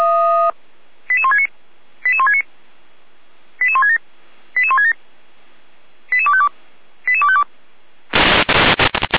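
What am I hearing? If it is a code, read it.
Five-tone selective call (ZVEI), heard as quick runs of five tones, tones of about 70 ms. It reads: 88169 88169 88167 88167 88252 88252
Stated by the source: ZVEI five-tone selective call received by SDR.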